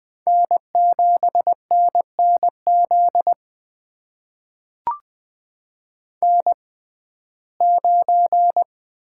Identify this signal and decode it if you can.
Morse code, heard as a keyed tone, about 20 wpm, spaced as for 14 wpm, 700 Hz. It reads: N7NNZ N9
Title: USA Call Signs - 5 Characters - 6x Character Spacing - 20wpm